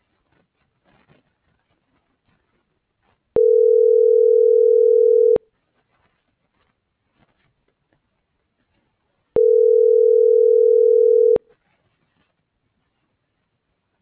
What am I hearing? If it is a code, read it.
Ringback tone (North America), 440 + 480 Hz.